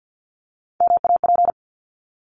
Morse code, heard as a keyed tone, about 50 wpm, 700 Hz. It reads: MAP
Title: Top 500 Words Repeated 50wpm